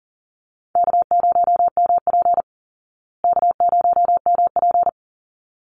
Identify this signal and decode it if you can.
Morse code, heard as a keyed tone, about 40 wpm, 700 Hz. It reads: K0MP K0MP
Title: Call Signs Repeated 3x - 40wpm